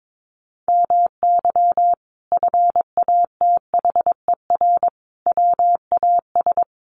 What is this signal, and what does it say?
Morse code, heard as a keyed tone, about 22 wpm, 700 Hz. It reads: MY FATHER WAS
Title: Sentences from Top 1400 Words 22wpm